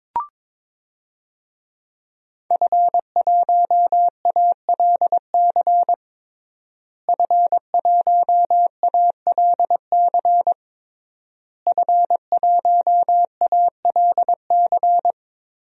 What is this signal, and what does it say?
Morse code, heard as a keyed tone, about 22 wpm, 700 Hz. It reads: F1ALC F1ALC F1ALC
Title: DXCC-100 Call Signs - 5 Characters - Repeated 3x - 22wpm